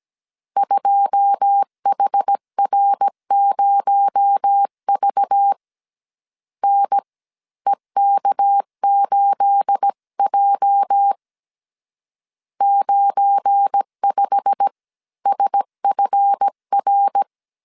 Morse code, heard as a keyed tone, about 17 wpm, spaced as for 14 wpm, 800 Hz. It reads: 2HR0V NEK8J 95SFR
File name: CW / DET / 1 9 1500C S B 16wpm 2856